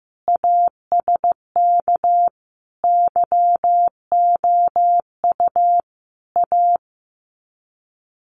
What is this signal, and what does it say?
Morse code, heard as a keyed tone, about 15 wpm, 700 Hz. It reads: ASK YOU A
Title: Sets of 5 Words Encore 15wpm